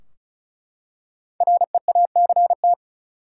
Morse code, heard as a keyed tone, about 34 wpm, 700 Hz. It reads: REACT
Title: Five Letter Words - 35wpm